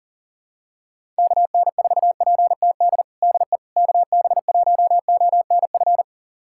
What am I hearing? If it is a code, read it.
Morse code, heard as a keyed tone, about 40 wpm, 700 Hz. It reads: KN4PTD DE KB1ONF
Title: USA Callsign DE Callsign - 6 Characters - 40wpm